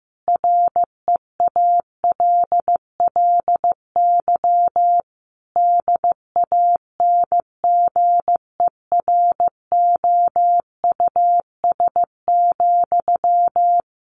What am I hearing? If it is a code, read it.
Morse code, heard as a keyed tone, about 15 wpm, 700 Hz. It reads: REALLY DANGEROUS,
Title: Sentences from Top 1200 Words 15wpm